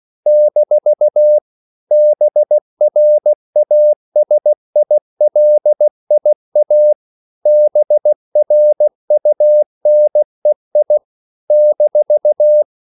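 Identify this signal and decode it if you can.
Morse code, heard as a keyed tone, about 16 wpm, 600 Hz. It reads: - BRASILIA BRUNEI -
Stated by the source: CW practice of national capitals at 16 words per minute, from ON6ZQ.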